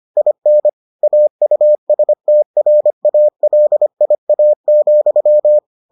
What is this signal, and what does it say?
Morse code, heard as a keyed tone, about 25 wpm, 600 Hz. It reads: IN AUSTRALIA,